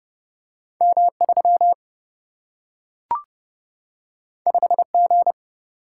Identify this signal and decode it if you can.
Morse code, heard as a keyed tone, about 30 wpm, 700 Hz. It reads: M 3 5 G
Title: Single-Letter Single-Number 30wpm